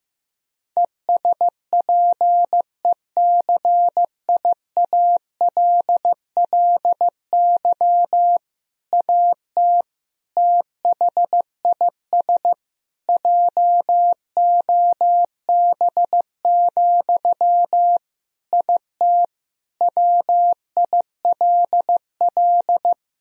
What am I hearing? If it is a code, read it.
Morse code, heard as a keyed tone, about 15 wpm, 700 Hz. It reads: ESPECIALLY AT THIS JOB, IT WILL
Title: Sentences from Top 1000 Words - Review Part 4 - 15wpm